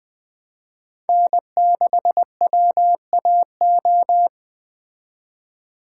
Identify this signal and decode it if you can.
Morse code, heard as a keyed tone, about 22 wpm, 700 Hz. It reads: N6WAO